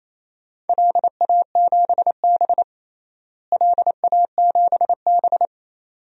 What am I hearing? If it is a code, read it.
Morse code, heard as a keyed tone, about 28 wpm, 700 Hz. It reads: LA7B LA7B
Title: DXCC-100 Call Signs - 4 Characters - Repeated 2x - 28wpm